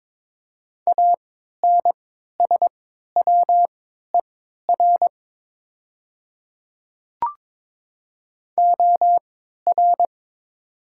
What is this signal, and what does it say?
Morse code, heard as a keyed tone, about 22 wpm, spaced as for 13 wpm, 700 Hz. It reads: ANSWER OR
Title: 3x Character Spacing - Top 300 Words - 22wpm